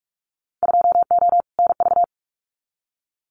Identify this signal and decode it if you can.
Morse code, heard as a keyed tone, about 44 wpm, 700 Hz. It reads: 2O NV
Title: Winter Field Day Exchanges 45wpm